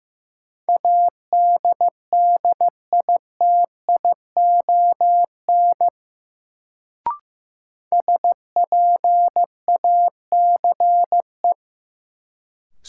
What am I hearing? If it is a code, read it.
Morse code, heard as a keyed tone, about 15 wpm, 700 Hz. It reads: ADDITION SPACE
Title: Sentences from Top 1500 Words - New Word Drill Alt - 15wpm